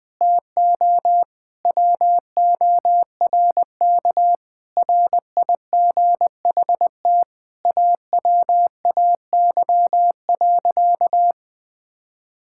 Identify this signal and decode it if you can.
Morse code, heard as a keyed tone, about 21 wpm, 700 Hz. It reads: TO WORK RIGHT AWAY.
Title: Sentences from Top 1600 Words 20wpm